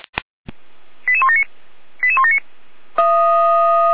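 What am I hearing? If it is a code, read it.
Five-tone selective call (ZVEI), heard as quick runs of five tones, tones of about 70 ms. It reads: 88179 88179
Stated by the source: ZVEI five-tone selective call 88074, received by SDR on the 455 channel.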